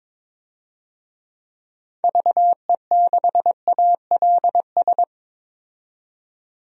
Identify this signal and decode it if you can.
Morse code, heard as a keyed tone, about 22 wpm, 700 Hz. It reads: VE6ALS
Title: DXCC-100 Call Signs - 6 Characters - 22wpm